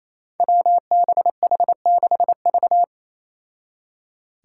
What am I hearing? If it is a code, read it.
Morse code, heard as a keyed tone, about 27 wpm, 700 Hz. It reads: WBH6V
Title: Five Challenging Letter-Number Combinations 28wpm